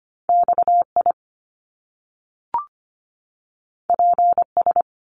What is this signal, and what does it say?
Morse code, heard as a keyed tone, about 25 wpm, 700 Hz. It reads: XI PS
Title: Top Two Letter Combinations in Words 25wpm